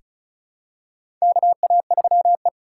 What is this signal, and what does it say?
Morse code, heard as a keyed tone, about 36 wpm, 700 Hz. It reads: KA3E
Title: USA Call Signs - 2x Character Spacing - 35wpm